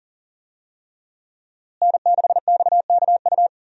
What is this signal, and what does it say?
Morse code, heard as a keyed tone, about 40 wpm, 700 Hz. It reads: N6XKU
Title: USA Call Signs - 5 Characters - Repeated 3x - 40wpm